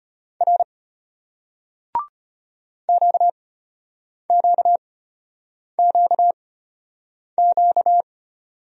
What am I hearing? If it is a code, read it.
Morse code, heard as a keyed tone, about 33 wpm, spaced as for 8 wpm, 700 Hz. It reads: R QQQQ